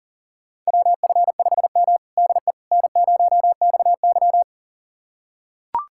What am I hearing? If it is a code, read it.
Morse code, heard as a keyed tone, about 40 wpm, 700 Hz. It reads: WF5M DE N0XY
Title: USA Callsign DE Callsign - 4 Characters - 40wpm